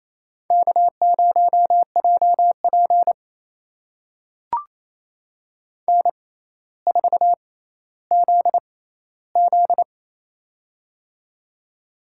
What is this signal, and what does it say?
Morse code, heard as a keyed tone, about 28 wpm, spaced as for 14 wpm, 700 Hz. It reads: K0JP N4ZZ
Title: USA Call Signs - 4 Characters - 6x Character Spacing - 28wpm